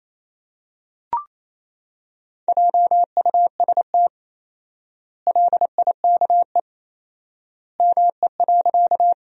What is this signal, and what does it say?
Morse code, heard as a keyed tone, about 28 wpm, 700 Hz. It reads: JUST LIKE ME.
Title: 4x Word-Spacing - Sentences from Top 200 Words - 28wpm